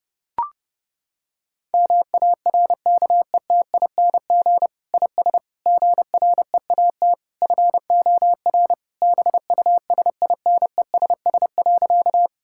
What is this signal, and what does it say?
Morse code, heard as a keyed tone, about 30 wpm, 700 Hz. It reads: MARKETING IS GREAT FOR BUSINESS.
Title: Sentences from Top 1000 Words - Review Part 5 - 30wpm